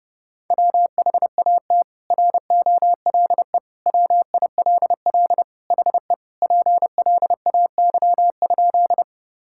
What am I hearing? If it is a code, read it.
Morse code, heard as a keyed tone, about 30 wpm, 700 Hz. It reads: WHAT ROLE WILL HE PLAY?
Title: Sentences from Top 1000 Words - Review Part 6 - 30wpm